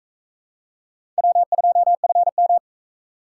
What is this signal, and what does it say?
Morse code, heard as a keyed tone, about 40 wpm, 700 Hz. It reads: W2FM